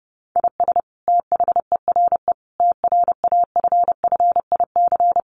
Morse code, heard as a keyed tone, about 30 wpm, 700 Hz. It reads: IS THERE TRAFFIC
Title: Sentences from Top 1300 Words 30wpm